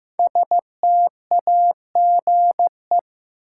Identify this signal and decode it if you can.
Morse code, heard as a keyed tone, about 15 wpm, 700 Hz. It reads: STAGE
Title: Sentences from Top 1200 Words - New Word Drill Alt - 15wpm